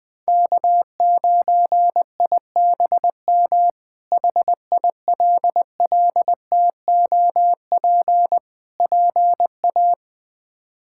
Morse code, heard as a keyed tone, about 20 wpm, 700 Hz. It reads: K9IBM HILLTOP PA
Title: K1USN SST QSO Practice - 20wpm